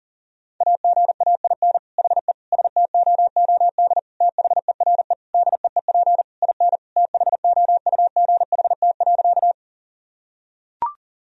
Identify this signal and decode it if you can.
Morse code, heard as a keyed tone, about 40 wpm, 700 Hz. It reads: AGAIN HE STOOD THERE DEEP IN THOUGHT.
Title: Sentences from Top 1000 Words - Review Part 3 - 40wpm